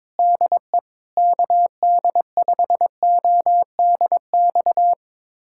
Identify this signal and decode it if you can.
Morse code, heard as a keyed tone, about 22 wpm, 700 Hz. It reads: DE KD5ODX
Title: USA Callsign DE Callsign - 6 Characters - 22wpm